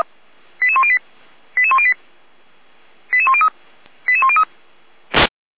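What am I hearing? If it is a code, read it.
Five-tone selective call (ZVEI), heard as quick runs of five tones, tones of about 70 ms. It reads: 88108 88108 88103 88103
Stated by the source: ZVEI five-tone selective call received with an SDR, fire brigade feuerwehr-orlamuende.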